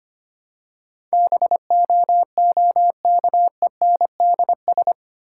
Morse code, heard as a keyed tone, about 25 wpm, 690 Hz. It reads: BOOKENDS
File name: practice / 25wpm / Easy Words with Hints - 25wpm